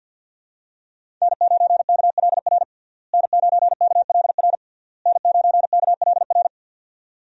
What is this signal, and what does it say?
Morse code, heard as a keyed tone, about 49 wpm, 700 Hz. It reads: N9KLR N9KLR N9KLR
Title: USA Call Signs - 5 Characters - Repeated 3x - 50wpm